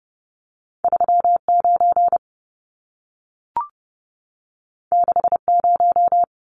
Morse code, heard as a keyed tone, about 30 wpm, 700 Hz. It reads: 39 60
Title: Numbers - 2-Digit 30wpm